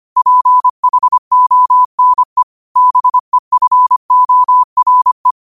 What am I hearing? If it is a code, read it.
Morse code, heard as a keyed tone, about 25 wpm, 1000 Hz. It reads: PHONE BEFORE
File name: qotdfiles / 25wpm